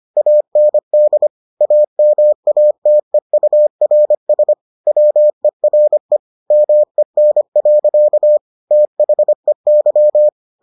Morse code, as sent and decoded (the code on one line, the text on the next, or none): .- -. -.. / .- -- .- - . ..- .-. ... / .-- . .-. / -- . -. .-.-.- / - .... -.--
AND AMATEURS WERE MEN. THEY